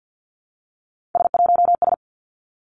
Morse code, heard as a keyed tone, about 50 wpm, 700 Hz. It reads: S1S
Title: Three Challenging Letter-Number Combinations 50wpm